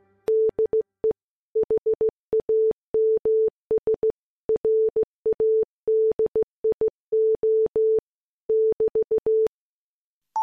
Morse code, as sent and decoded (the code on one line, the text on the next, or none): -.. / .... .- -- ... / .-. .- -.. .. --- / -...-
DE HAMS RADIO <BT>